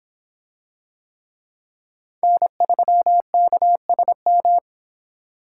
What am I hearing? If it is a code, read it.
Morse code, heard as a keyed tone, about 26 wpm, 700 Hz. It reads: N3KSM